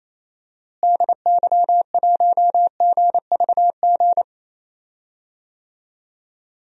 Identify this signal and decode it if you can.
Morse code, heard as a keyed tone, about 28 wpm, 700 Hz. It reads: DY1GVG
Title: DXCC-100 Call Signs - 6 Characters - Repeated 2x - 28wpm